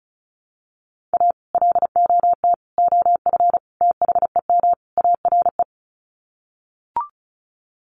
Morse code, heard as a keyed tone, about 34 wpm, 700 Hz. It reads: A LOT OF THEM ARE
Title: Sets of 5 Words Encore2 35wpm